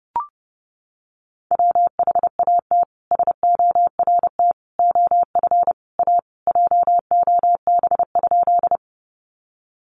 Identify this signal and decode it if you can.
Morse code, heard as a keyed tone, about 30 wpm, 700 Hz. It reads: WHAT SORT OF A JOB?
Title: Sentences from Top 1300 Words 30wpm